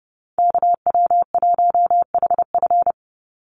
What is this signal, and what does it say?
Morse code, heard as a keyed tone, about 30 wpm, 700 Hz. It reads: KW1HF
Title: USA Call Signs 30wpm